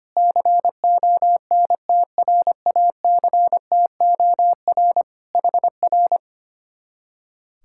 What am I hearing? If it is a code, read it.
Morse code, heard as a keyed tone, about 24 wpm, 700 Hz. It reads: CONTRACTOR HR
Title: Speed-Racing Common QSO Phrases - 25wpm